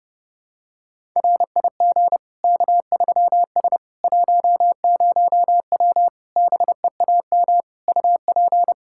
Morse code, heard as a keyed tone, about 30 wpm, 700 Hz. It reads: RIG K3S 10W BEAM UP
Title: Practice QSO - Exchange 2 - Info - 30wpm